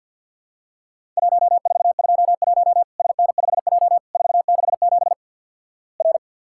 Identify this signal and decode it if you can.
Morse code, heard as a keyed tone, about 48 wpm, 700 Hz. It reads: 1421 SN5J 467 R
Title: Worked All Europe DX Contest - QTC Exchange with Repeats - 50wpm